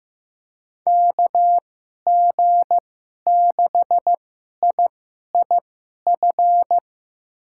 Morse code, heard as a keyed tone, about 15 wpm, spaced as for 11 wpm, 700 Hz. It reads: KG6IIF
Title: USA Call Signs - 2x Character Spacing - 15wpm